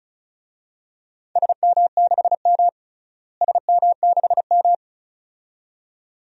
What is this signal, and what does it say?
Morse code, heard as a keyed tone, about 34 wpm, 700 Hz. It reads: SM6M SM6M
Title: DXCC-100 Call Signs - 4 Characters - Repeated 2x - 35wpm